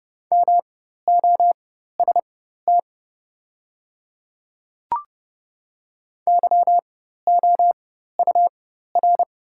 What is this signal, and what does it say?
Morse code, heard as a keyed tone, about 30 wpm, spaced as for 14 wpm, 700 Hz. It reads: MOST YOUR
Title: 4x Character Spacing - Top 100 Words - 30wpm